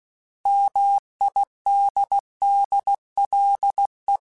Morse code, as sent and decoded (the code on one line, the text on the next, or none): -- .. -.. -.. .-.. .
MIDDLE